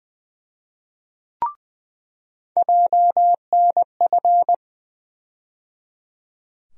Morse code, heard as a keyed tone, about 20 wpm, 700 Hz. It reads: JNF
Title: USA Call Sign Postfixes - 3 Letters 20wpm